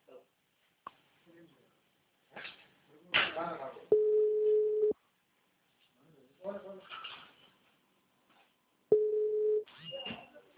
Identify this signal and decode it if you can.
Ringback tone (Europe, 425 Hz), with speech.